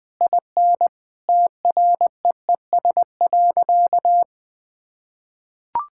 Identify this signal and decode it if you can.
Morse code, heard as a keyed tone, about 19 wpm, 700 Hz. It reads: IN TREES.
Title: Easy Sentences with Hints - 20wpm